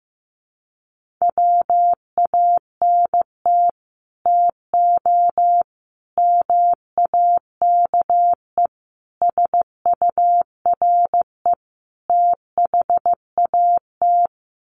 Morse code, heard as a keyed tone, about 15 wpm, 700 Hz. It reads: WANT TO MAKE SURE THAT
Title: Sets of 5 Words 15wpm